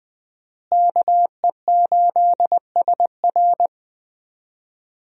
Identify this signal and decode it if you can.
Morse code, heard as a keyed tone, about 20 wpm, 700 Hz. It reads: KE8SR
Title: USA Call Signs - 5 Characters - Repeated 3x - 20wpm